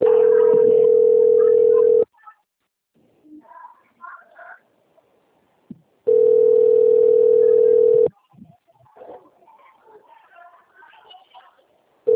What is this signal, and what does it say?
Ringback tone (North America), 440 + 480 Hz.
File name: out- - - - - .50718